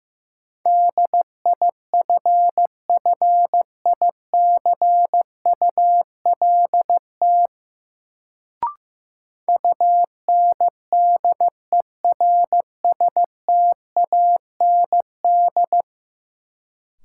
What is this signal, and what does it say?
Morse code, heard as a keyed tone, about 15 wpm, 700 Hz. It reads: DIFFICULT UNDERSTAND
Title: Sentences from Top 600 Words - New Word Drill - 15wpm